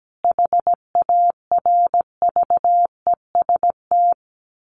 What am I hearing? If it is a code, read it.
Morse code, heard as a keyed tone, about 17 wpm, 700 Hz. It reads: HARVEST
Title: A Sound Of Thunder 17wpm